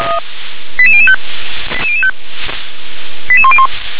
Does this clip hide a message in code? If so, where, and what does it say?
Yes: five-tone selective call (ZVEI) from 0.8 to 3.7 s, heard as quick runs of five tones, tones of about 70 ms. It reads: 88005 88005 88101